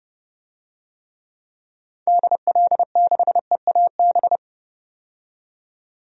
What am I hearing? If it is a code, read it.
Morse code, heard as a keyed tone, about 31 wpm, 700 Hz. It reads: DL6EAB
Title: DXCC-100 Call Signs - 6 Characters - 30wpm